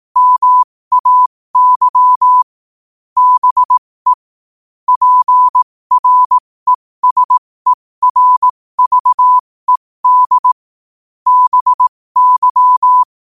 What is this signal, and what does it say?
Morse code, heard as a keyed tone, about 18 wpm, spaced as for 15 wpm, 1000 Hz. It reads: MAY BE PRESERVED BY
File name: qotdfiles / 15wpm